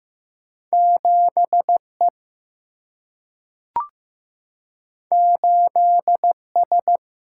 Morse code, heard as a keyed tone, about 15 wpm, 700 Hz. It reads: 7E 8S
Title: Single-Letter Single-Number 15wpm